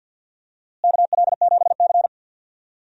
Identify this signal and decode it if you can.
Morse code, heard as a keyed tone, about 50 wpm, 700 Hz. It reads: KL7C
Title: DXCC-100 Call Signs - 4 Characters - Repeated 2x - 50wpm